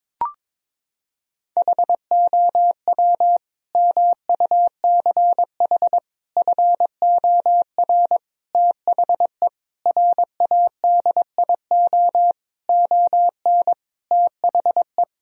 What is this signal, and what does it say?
Morse code, heard as a keyed tone, about 22 wpm, 700 Hz. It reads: HOW MUCH FOR THE RADIO ON THE
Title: Sentences from Top 1000 Words - Review Part 3 - 22wpm